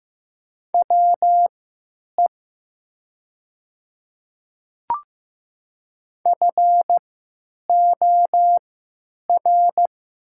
Morse code, heard as a keyed tone, about 15 wpm, spaced as for 9 wpm, 700 Hz. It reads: WE FOR
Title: 3x Character Spacing - Top 200 Words - 15wpm